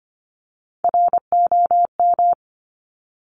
Morse code, heard as a keyed tone, about 25 wpm, 700 Hz. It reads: ROM
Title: Top Three Letter Combinations in Words 25wpm